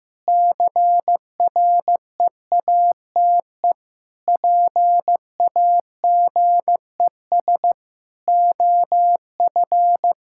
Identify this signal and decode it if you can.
Morse code, heard as a keyed tone, about 15 wpm, 700 Hz. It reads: CREATE PAGES OF
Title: Pangram Challenge v2 - 15wpm